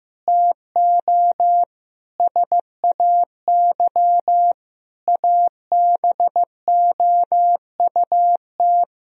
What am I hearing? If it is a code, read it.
Morse code, heard as a keyed tone, about 15 wpm, 700 Hz. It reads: TO SAY ABOUT
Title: Sentences from Top 1000 Words - Review Part 3 - 15wpm